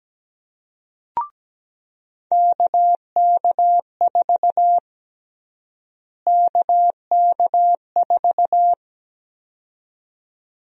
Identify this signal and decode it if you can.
Morse code, heard as a keyed tone, about 17 wpm, 700 Hz. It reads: KK4 KK4
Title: USA Call Sign Prefixes - Repeated 2x - 17wpm